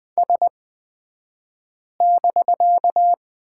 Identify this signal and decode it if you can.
Morse code, heard as a keyed tone, about 20 wpm, spaced as for 5 wpm, 700 Hz. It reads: S<BK>
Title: 249 Lesson 40d - BK - Two Characters 20wpm